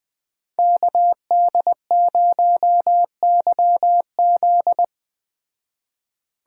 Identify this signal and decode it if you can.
Morse code, heard as a keyed tone, about 20 wpm, 700 Hz. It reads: KD0YZ